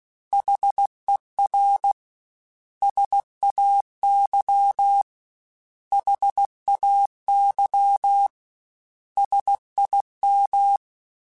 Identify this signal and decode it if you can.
Morse code, heard as a keyed tone, about 16 wpm, 800 Hz. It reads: HER SAY HAY SIM
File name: S vs H / S vs H mp3s / S vs H @15